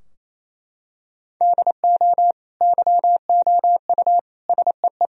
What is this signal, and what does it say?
Morse code, heard as a keyed tone, about 28 wpm, 700 Hz. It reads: DO YOU SEE
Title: Sentences from Top 1000 Words - Review Part 4 - 28wpm